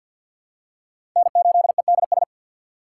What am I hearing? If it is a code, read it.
Morse code, heard as a keyed tone, about 49 wpm, 700 Hz. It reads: N8EDS
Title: USA Call Signs - 5 Characters - 4x Character Spacing - 50wpm